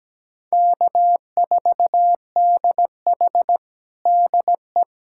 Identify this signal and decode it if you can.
Morse code, heard as a keyed tone, about 17 wpm, 700 Hz. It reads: K4DH DE
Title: USA Callsign DE Callsign 17wpm